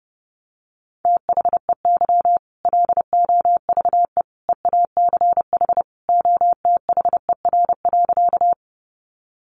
Morse code, heard as a keyed tone, about 30 wpm, 700 Hz. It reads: THEY LOVE EACH OTHER.
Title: Easy Phrases 30wpm